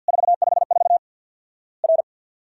Morse code, heard as a keyed tone, about 48 wpm, 700 Hz. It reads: 454 R
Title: Worked All Europe DX Contest - QTC Exchange with Repeats - 50wpm